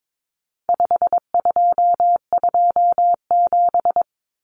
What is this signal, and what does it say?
Morse code, heard as a keyed tone, about 22 wpm, 700 Hz. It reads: 5227
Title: Common Cut Numbers - 4-Digit 22wpm